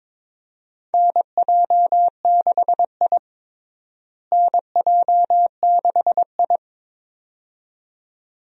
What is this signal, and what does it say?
Morse code, heard as a keyed tone, about 22 wpm, 700 Hz. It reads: NJ6I NJ6I
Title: USA Call Signs - 4 Characters - Repeated 2x - 22wpm